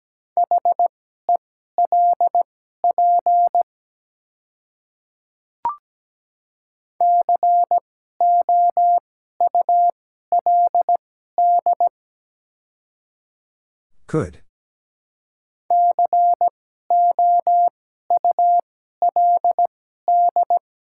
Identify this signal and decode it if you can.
Morse code, heard as a keyed tone, about 17 wpm, spaced as for 12 wpm, 700 Hz. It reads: HELP COULD COULD
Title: 2x Character Spacing - Top 200 Words - 17wpm